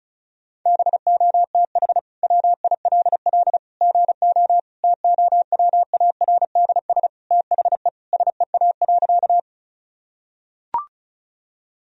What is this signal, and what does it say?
Morse code, heard as a keyed tone, about 35 wpm, 700 Hz. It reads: BOTH WILL GO TOWARDS THE SEA.
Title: Sentences from Top 1000 Words - Review Part 2 - 35wpm